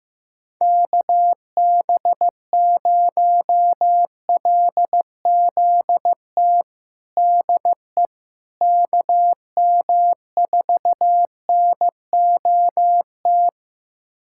Morse code, heard as a keyed tone, about 15 wpm, 700 Hz. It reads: KB0LZT DE KM4NOT
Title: USA Callsign DE Callsign - 6 Characters - 15wpm